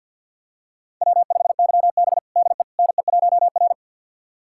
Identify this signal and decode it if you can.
Morse code, heard as a keyed tone, about 51 wpm, 700 Hz. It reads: W5YB DE NE1R
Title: USA Callsign DE Callsign - 4 Characters - 50wpm